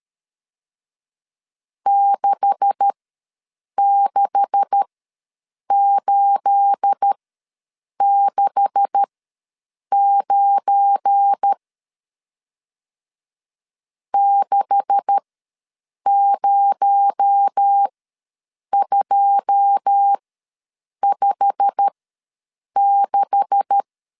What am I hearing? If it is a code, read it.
Morse code, heard as a keyed tone, about 13 wpm, spaced as for 7 wpm, 800 Hz. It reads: 66869 60256
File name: LEZIONI / 9 286 ws 12wpm 3351